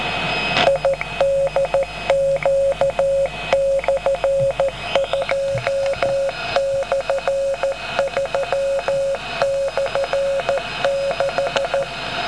Morse code, heard as a keyed tone, about 13 wpm, 550 Hz, with noise. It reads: IDQ/2/3/6